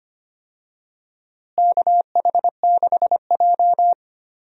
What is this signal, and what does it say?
Morse code, heard as a keyed tone, about 25 wpm, 700 Hz. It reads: KH6J
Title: USA Call Signs - 4 Characters - Repeated 3x - 25wpm